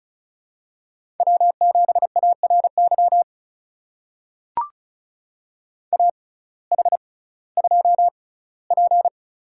Morse code, heard as a keyed tone, about 36 wpm, spaced as for 19 wpm, 700 Hz. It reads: W7ARY AH2P
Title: USA Call Signs - 6x Character Spacing - 35wpm